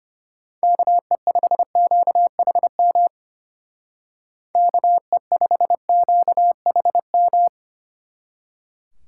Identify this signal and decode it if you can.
Morse code, heard as a keyed tone, about 30 wpm, 700 Hz. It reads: KE5QHM KE5QHM
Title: Speed Racing - USA 2x3 Call Signs 25wpm